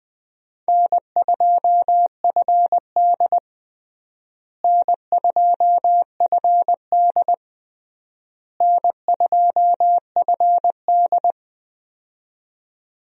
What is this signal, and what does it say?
Morse code, heard as a keyed tone, about 21 wpm, 700 Hz. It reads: N2FD N2FD N2FD
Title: USA Call Signs - 4 Characters - Repeated 3x - 20wpm